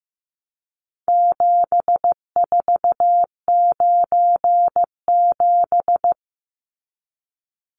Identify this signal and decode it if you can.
Morse code, heard as a keyed tone, about 15 wpm, 700 Hz. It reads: 7497
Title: Numbers - 4-Digit 15wpm